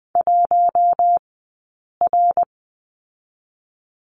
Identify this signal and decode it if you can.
Morse code, heard as a keyed tone, about 20 wpm, 700 Hz. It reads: 1 R
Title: 052 Lesson 10d - Letter r - Two Characters 20wpm